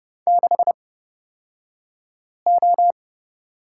Morse code, heard as a keyed tone, about 30 wpm, spaced as for 5 wpm, 700 Hz. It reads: 6O
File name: Lesson 37d - Letter x - Two Characters 30wpm